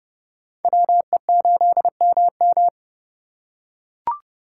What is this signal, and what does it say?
Morse code, heard as a keyed tone, about 30 wpm, 700 Hz. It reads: WE8MM